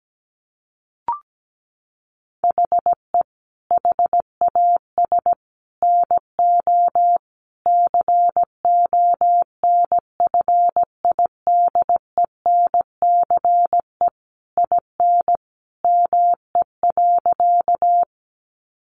Morse code, heard as a keyed tone, about 17 wpm, 700 Hz. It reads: HE HAS NO CONFIDENCE IN ME.
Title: Sentences from Top 1500 Words 17wpm